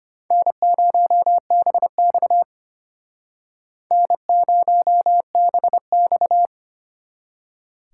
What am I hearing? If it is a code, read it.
Morse code, heard as a keyed tone, about 32 wpm, 700 Hz. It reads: N0BX N0BX